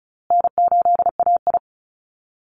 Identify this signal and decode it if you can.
Morse code, heard as a keyed tone, about 34 wpm, 700 Hz. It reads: N8AI
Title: Call Signs Repeated 2x - 35wpm